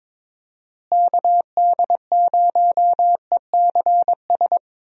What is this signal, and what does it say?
Morse code, heard as a keyed tone, about 22 wpm, 700 Hz. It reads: KD0ECS
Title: USA Call Signs - 6 Characters - Repeated 3x - 22wpm